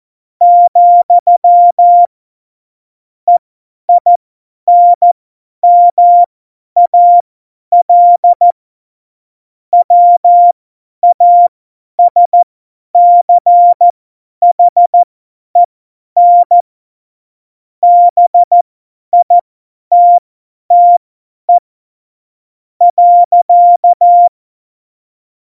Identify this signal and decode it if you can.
Morse code, heard as a keyed tone, about 14 wpm, spaced as for 10 wpm, 700 Hz. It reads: , EINMAL WASCHEN BITTE.
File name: Morsekurs / Witze / Witz 12. Februar 2019, 50 F70